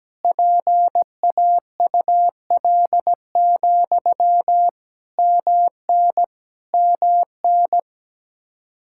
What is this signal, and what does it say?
Morse code, heard as a keyed tone, about 17 wpm, 700 Hz. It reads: PAUL, MN MN
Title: Practice QSO - Exchange 1 - Beginning - 17wpm